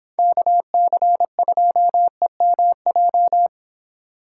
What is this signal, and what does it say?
Morse code, heard as a keyed tone, about 30 wpm, 700 Hz. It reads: KC2EMJ